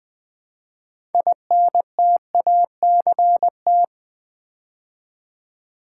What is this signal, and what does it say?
Morse code, heard as a keyed tone, about 20 wpm, 690 Hz. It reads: INTACT
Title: Build A Word - 3 Letters At A Time - 6-Letter Words - 20wpm